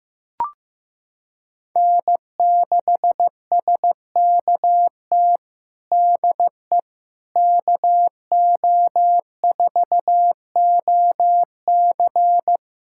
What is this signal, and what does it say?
Morse code, heard as a keyed tone, about 15 wpm, 700 Hz. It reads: N6SKT DE KO4OC
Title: USA Callsign DE Callsign - 5 Characters - 15wpm